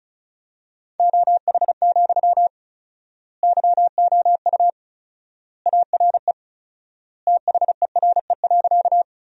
Morse code, heard as a keyed tone, about 34 wpm, 700 Hz. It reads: OH, YOU ARE THERE.
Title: 4x Word-Spacing - Sentences from Top 500 Words - 35wpm